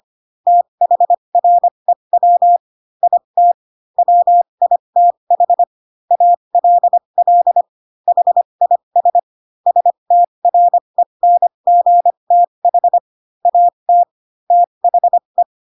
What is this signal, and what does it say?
Morse code, heard as a keyed tone, about 25 wpm, spaced as for 22 wpm, 700 Hz. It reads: THREW IT WITH ALL HIS STRENGTH AT THE
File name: Pinocchio / Pinocchio ch4 18 25 ik1hge